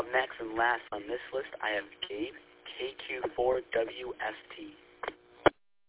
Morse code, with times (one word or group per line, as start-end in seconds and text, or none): none